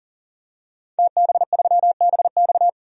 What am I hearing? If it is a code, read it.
Morse code, heard as a keyed tone, about 40 wpm, 700 Hz. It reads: TB3BX
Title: DXCC-100 Call Signs - 5 Characters - 40wpm